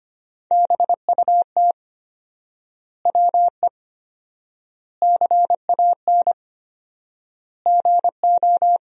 Morse code, heard as a keyed tone, about 25 wpm, 700 Hz. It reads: BUT WE CAN GO
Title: 4x Word-Spacing - Sentences from Top 100 Words - 25wpm